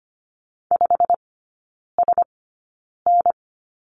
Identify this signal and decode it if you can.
Morse code, heard as a keyed tone, about 25 wpm, spaced as for 9 wpm, 690 Hz. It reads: 5SN
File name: Lesson 15e - Number 5 - Three Characters 25wpm